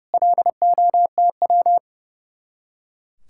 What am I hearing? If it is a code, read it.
Morse code, heard as a keyed tone, about 36 wpm, 700 Hz. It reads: LOTW
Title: Speed-Racing Common QSO Phrases Encore - 30wpm